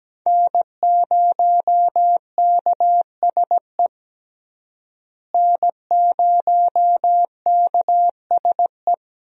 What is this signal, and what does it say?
Morse code, heard as a keyed tone, about 17 wpm, 700 Hz. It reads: N0KSE N0KSE
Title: USA Call Signs - 5 Characters - Repeated 2x - 17wpm